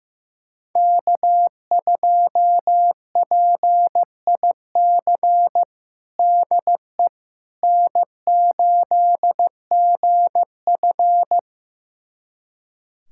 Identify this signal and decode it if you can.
Morse code, heard as a keyed tone, about 15 wpm, 700 Hz. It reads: K2PIC DE N8GF
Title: USA Callsign DE Callsign 15wpm